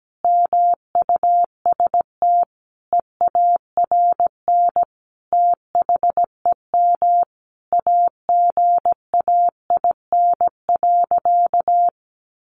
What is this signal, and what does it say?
Morse code, heard as a keyed tone, about 17 wpm, 700 Hz. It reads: MUST EARN THEM AGAIN.
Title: Sentences from Top 1500 Words 17wpm